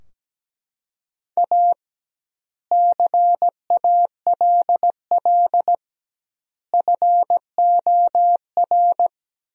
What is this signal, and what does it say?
Morse code, heard as a keyed tone, about 17 wpm, 700 Hz. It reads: A CALL FOR
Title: 2x Word-Spacing - Sentences from Top 300 Words - 17wpm